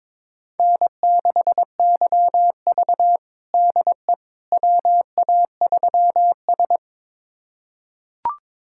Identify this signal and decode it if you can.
Morse code, heard as a keyed tone, about 22 wpm, 700 Hz. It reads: N6YV DE WA3S
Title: USA Callsign DE Callsign - 4 Characters - 22wpm